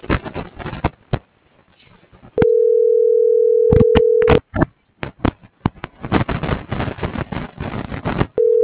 Ringback tone (North America), 440 + 480 Hz.